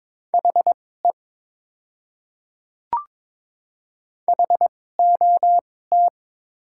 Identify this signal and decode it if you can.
Morse code, heard as a keyed tone, about 22 wpm, spaced as for 16 wpm, 700 Hz. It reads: HE HOT
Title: 2x Character Spacing - Top 200 Words - 22wpm